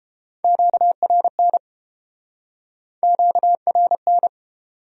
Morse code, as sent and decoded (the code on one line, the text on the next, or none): --.- .-. -. / --.- .-. -.
QRN QRN